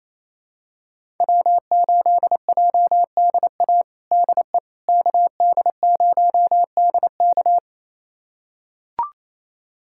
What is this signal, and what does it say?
Morse code, heard as a keyed tone, about 28 wpm, 700 Hz. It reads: W8JDA DE KD0DK
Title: USA Callsign DE Callsign - 5 Characters - 28wpm